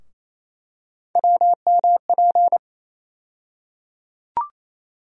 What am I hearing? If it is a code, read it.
Morse code, heard as a keyed tone, about 27 wpm, 700 Hz. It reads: WMP
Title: USA Call Sign Postfixes - 3 Letters 28wpm